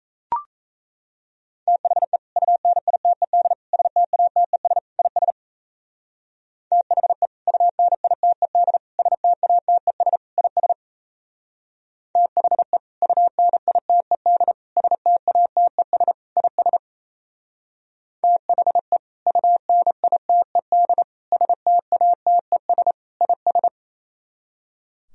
Morse code, heard as a keyed tone, about 35 wpm, 700 Hz. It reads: THE UNITED STATES IS THE UNITED STATES IS THE UNITED STATES IS THE UNITED STATES IS
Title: Speed-Racing - Sets of 4 Words 28wpm